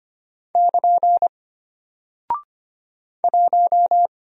Morse code, heard as a keyed tone, about 25 wpm, 700 Hz. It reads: <KN> 1
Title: Single Character (Letter:Number:Punctuation:Prosign) Alt 25wpm